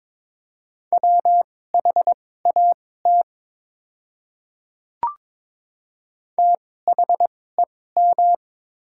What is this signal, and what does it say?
Morse code, heard as a keyed tone, about 22 wpm, spaced as for 16 wpm, 700 Hz. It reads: WHAT THEM
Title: 2x Character Spacing - Top 100 Words - 22wpm